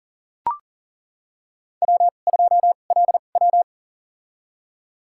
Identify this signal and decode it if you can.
Morse code, heard as a keyed tone, about 37 wpm, spaced as for 28 wpm, 700 Hz. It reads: W2LW
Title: USA Call Signs - 4 Characters - 2x Character Spacing - 40wpm